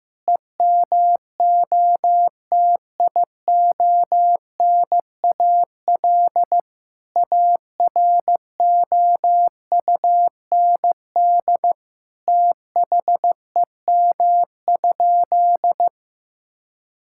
Morse code, heard as a keyed tone, about 15 wpm, 700 Hz. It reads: EMOTIONAL AROUND THEM?
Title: Sentences from Top 1000 Words - Review Part 5 - 15wpm